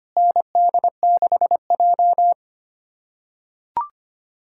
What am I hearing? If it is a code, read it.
Morse code, heard as a keyed tone, about 25 wpm, 700 Hz. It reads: ND6J
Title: USA Callsign DE Callsign - 4 Characters - 25wpm